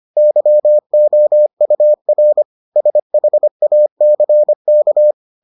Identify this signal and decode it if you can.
Morse code, heard as a keyed tone, about 25 wpm, 600 Hz. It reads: YOUR SHACK